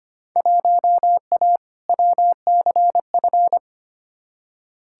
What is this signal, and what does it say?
Morse code, heard as a keyed tone, about 25 wpm, 700 Hz. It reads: 1A WCF
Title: Field Day Exchanges - 25wpm